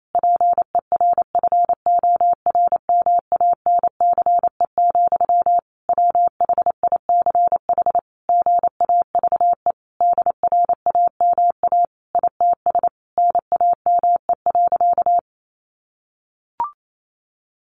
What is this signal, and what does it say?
Morse code, heard as a keyed tone, about 28 wpm, 700 Hz. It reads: PERFORMANCE, WHICH GAVE DRAMA ITS NAME.
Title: Sentences from Top 1500 Words 28wpm